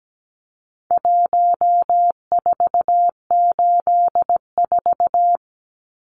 Morse code, heard as a keyed tone, about 17 wpm, 700 Hz. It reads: 1484
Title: Numbers - Contest Style - 4-Digit 17wpm